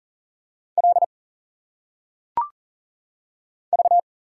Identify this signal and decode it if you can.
Morse code, heard as a keyed tone, about 38 wpm, 700 Hz. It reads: L V
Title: Single Letters - Phonetic Pronunciation 40wpm